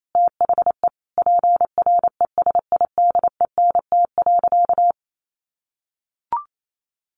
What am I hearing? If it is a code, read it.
Morse code, heard as a keyed tone, about 28 wpm, 700 Hz. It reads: THE PRESIDENT.
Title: Sentences from Top 1500 Words 28wpm